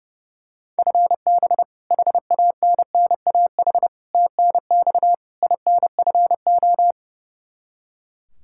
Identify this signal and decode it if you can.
Morse code, heard as a keyed tone, about 30 wpm, 700 Hz. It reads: FB HANNAH TNX INFO
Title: Practice QSO - Exchange 3 - More Info - 30wpm